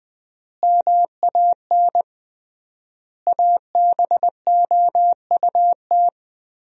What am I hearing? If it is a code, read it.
Morse code, heard as a keyed tone, about 20 wpm, 700 Hz. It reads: MAN ABOUT
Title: 3x Word-Spacing - Sentences from Top 400 Words - 20wpm